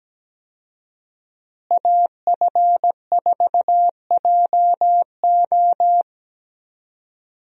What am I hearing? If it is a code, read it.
Morse code, heard as a keyed tone, about 17 wpm, 700 Hz. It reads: AF4JO
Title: USA Call Signs - 5 Characters - Repeated 3x - 17wpm